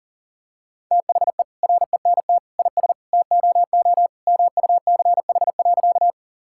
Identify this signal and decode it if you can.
Morse code, heard as a keyed tone, about 40 wpm, 700 Hz. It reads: THE RENT IS TOO MUCH.
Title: Sentences from Top 1000 Words - Review Part 5 - 40wpm